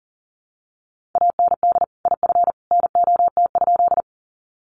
Morse code, heard as a keyed tone, about 40 wpm, 700 Hz. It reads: AND IF NOT?
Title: Sentences from Top 600 Words 40wpm